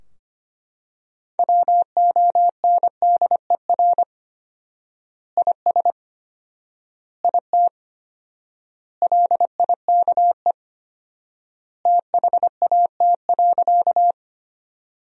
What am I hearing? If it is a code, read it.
Morse code, heard as a keyed tone, about 25 wpm, 700 Hz. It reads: WONDER IS IT LIKE THAT.
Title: 4x Word-Spacing - Sentences from Top 500 Words - 25wpm